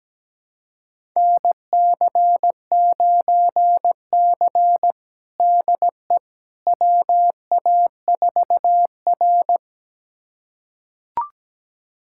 Morse code, heard as a keyed tone, about 17 wpm, 700 Hz. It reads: NC9C DE WA4R
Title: USA Callsign DE Callsign - 4 Characters - 17wpm